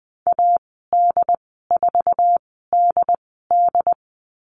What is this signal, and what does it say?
Morse code, heard as a keyed tone, about 20 wpm, spaced as for 14 wpm, 700 Hz. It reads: AD4DD
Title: 069 Lesson 12g - Letter d - Call Signs 20wpm